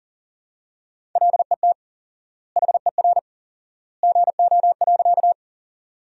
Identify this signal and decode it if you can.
Morse code, heard as a keyed tone, about 40 wpm, 700 Hz. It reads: LET HER GO.
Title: 4x Word-Spacing - Sentences from Top 300 Words - 40wpm